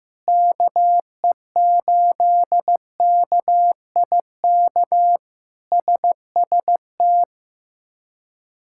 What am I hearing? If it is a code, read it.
Morse code, heard as a keyed tone, about 16 wpm, 700 Hz. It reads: KE8KIK SST
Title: K1USN SST QSO Practice - 15wpm